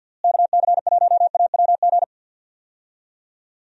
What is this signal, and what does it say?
Morse code, heard as a keyed tone, about 47 wpm, 700 Hz. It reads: KC1AWG
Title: USA Call Signs - 6 Characters - Repeated 3x - 50wpm